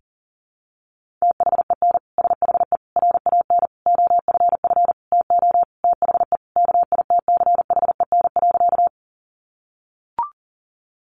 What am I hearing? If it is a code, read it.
Morse code, heard as a keyed tone, about 37 wpm, 700 Hz. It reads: THEN SHE RAN OFF TO THE KITCHEN.
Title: Sentences from Top 1400 Words 40wpm